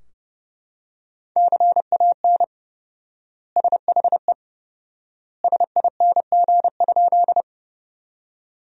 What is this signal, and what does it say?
Morse code, heard as a keyed tone, about 31 wpm, 700 Hz. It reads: CAN SHE SING?
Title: 4x Word-Spacing - Sentences from Top 400 Words - 30wpm